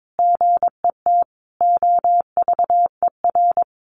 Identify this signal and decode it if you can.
Morse code, heard as a keyed tone, about 22 wpm, 700 Hz. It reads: GET OVER
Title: Sentences from Top 900 Words 22wpm